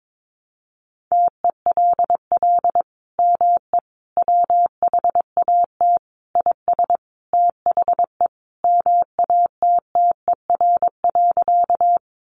Morse code, heard as a keyed tone, about 22 wpm, 700 Hz. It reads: TELL ME WHAT IS THE MATTER.
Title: Sentences from Top 700 Words 22wpm